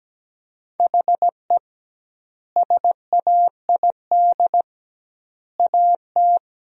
Morse code, heard as a keyed tone, about 17 wpm, 700 Hz. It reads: HE SAID AT
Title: 2x Word-Spacing - Sentences from Top 300 Words - 17wpm